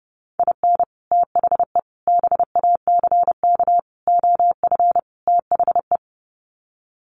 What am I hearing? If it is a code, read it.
Morse code, heard as a keyed tone, about 30 wpm, 700 Hz. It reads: IN THE BACK OF THE
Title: Sets of 5 Words 30wpm